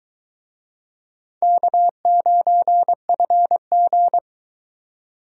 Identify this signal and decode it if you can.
Morse code, heard as a keyed tone, about 22 wpm, 700 Hz. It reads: K9FG